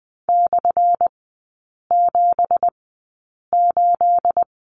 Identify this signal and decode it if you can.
Morse code, heard as a keyed tone, about 20 wpm, spaced as for 8 wpm, 700 Hz. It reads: /78
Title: 218 Lesson 35e - Number 0 - Three Characters 20wpm